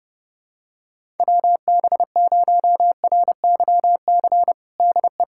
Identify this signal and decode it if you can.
Morse code, heard as a keyed tone, about 32 wpm, 700 Hz. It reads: WB0RYC DE
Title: USA Callsign DE Callsign 30wpm